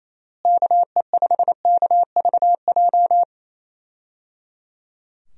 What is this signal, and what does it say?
Morse code, heard as a keyed tone, about 28 wpm, 700 Hz. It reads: KE5KVJ
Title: USA Call Signs - 6 Characters - Repeated 3x - 28wpm